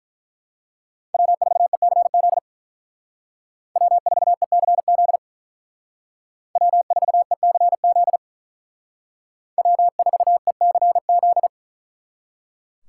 Morse code, heard as a keyed tone, about 43 wpm, 700 Hz. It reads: W4ECZ W4ECZ W4ECZ W4ECZ
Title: Speed Racing - USA 1x3 Call Signs 35wpm